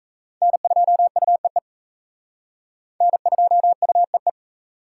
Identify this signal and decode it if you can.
Morse code, heard as a keyed tone, about 34 wpm, 700 Hz. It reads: N2UEE N2UEE
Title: Speed Racing - USA 1x3 Call Signs 28wpm